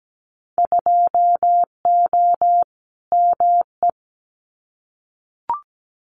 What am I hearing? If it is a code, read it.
Morse code, heard as a keyed tone, about 17 wpm, 700 Hz. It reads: 2O ME
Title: Winter Field Day Exchanges Encore 17wpm